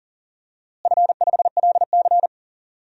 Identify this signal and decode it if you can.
Morse code, heard as a keyed tone, about 40 wpm, 700 Hz. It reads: F5LC
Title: DXCC-100 Call Signs - 4 Characters - Repeated 3x - 40wpm